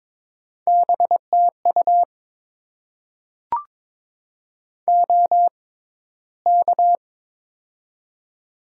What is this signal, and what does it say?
Morse code, heard as a keyed tone, about 23 wpm, spaced as for 14 wpm, 700 Hz. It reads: BTU OK
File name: Top 90 QSO Elements - 6x Character Spacing - 22wpm